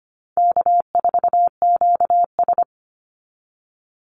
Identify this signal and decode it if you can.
Morse code, heard as a keyed tone, about 25 wpm, 700 Hz. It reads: K4QS
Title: NA SOTA Activator Callsigns 25wpm